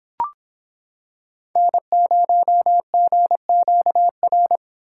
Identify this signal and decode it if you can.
Morse code, heard as a keyed tone, about 26 wpm, 700 Hz. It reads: N0GQR